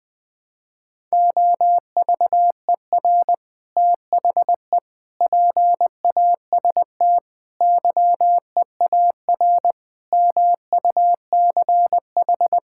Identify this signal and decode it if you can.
Morse code, heard as a keyed tone, about 20 wpm, 700 Hz. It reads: OVER THE PAST YEAR MUCH
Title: Sentences with 3-Word Phrases - Spoken Beginning - 20wpm